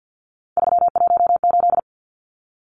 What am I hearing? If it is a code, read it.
Morse code, heard as a keyed tone, about 50 wpm, 700 Hz. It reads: 318
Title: Numbers - 3-Digit 50wpm